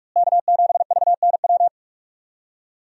Morse code, heard as a keyed tone, about 46 wpm, 700 Hz. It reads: K7VNW
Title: USA Call Signs - 5 Characters - Repeated 2x - 45wpm